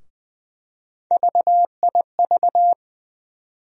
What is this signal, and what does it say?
Morse code, heard as a keyed tone, about 21 wpm, 700 Hz. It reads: VIV
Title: USA Call Sign Postfixes - 3 Letters 20wpm